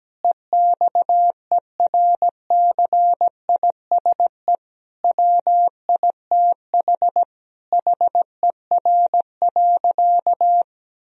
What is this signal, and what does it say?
Morse code, heard as a keyed tone, about 17 wpm, 700 Hz. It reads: EXERCISE WITH HER.
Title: Sentences from Top 1000 Words - Review Part 4 - 17wpm